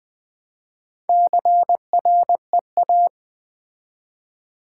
Morse code, heard as a keyed tone, about 20 wpm, 700 Hz. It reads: CREA
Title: Build A Word - 2 Letters At A Time - 8-Letter Words - 20wpm